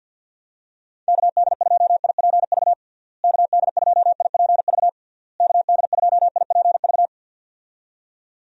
Morse code, heard as a keyed tone, about 48 wpm, 700 Hz. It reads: KD2IPV KD2IPV KD2IPV
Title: USA Call Signs - 6 Characters - Repeated 3x - 50wpm